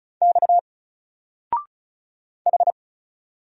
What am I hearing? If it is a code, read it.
Morse code, heard as a keyed tone, about 35 wpm, 700 Hz. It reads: X H
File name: Single-Letter-Number-35wpm